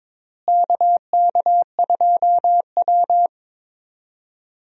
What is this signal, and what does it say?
Morse code, heard as a keyed tone, about 22 wpm, 700 Hz. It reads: KK2W
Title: USA Call Signs - 4 Characters - Repeated 2x - 22wpm